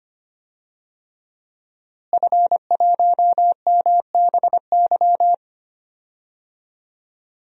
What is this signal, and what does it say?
Morse code, heard as a keyed tone, about 25 wpm, 700 Hz. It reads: F1MBY
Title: DXCC-100 Call Signs - 5 Characters - Repeated 3x - 25wpm